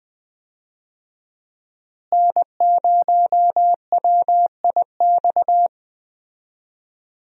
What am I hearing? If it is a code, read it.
Morse code, heard as a keyed tone, about 20 wpm, 700 Hz. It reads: N0WIX
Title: USA Call Signs - 5 Characters - Repeated 3x - 20wpm